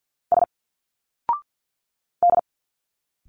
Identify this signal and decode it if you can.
Morse code, heard as a keyed tone, about 51 wpm, 700 Hz. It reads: S D
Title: Single Letters 50wpm